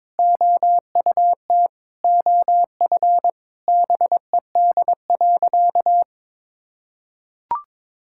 Morse code, heard as a keyed tone, about 22 wpm, 700 Hz. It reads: OUT OF BED.
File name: practice / 22wpm / Sentences with 3-Word Phrases - 22wpm